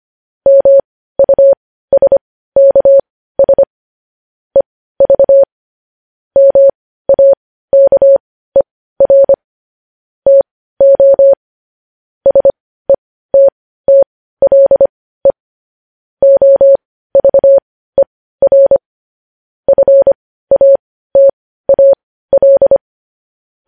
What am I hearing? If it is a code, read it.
Morse code, heard as a keyed tone, about 25 wpm, spaced as for 15 wpm, 550 Hz. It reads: MUSKS EV MAKER TO SETTLE OVER FATAL
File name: podcast / News headlines-1x-25x15